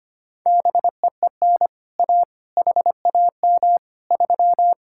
Morse code, heard as a keyed tone, about 25 wpm, 700 Hz. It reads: BEEN A HAM 3
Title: Speed-Racing Common QSO Phrases - 25wpm